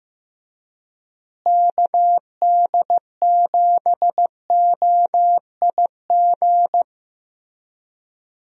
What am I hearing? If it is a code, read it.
Morse code, heard as a keyed tone, about 15 wpm, 700 Hz. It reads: KD7OIG
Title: USA Call Signs - 6 Characters - Repeated 3x - 15wpm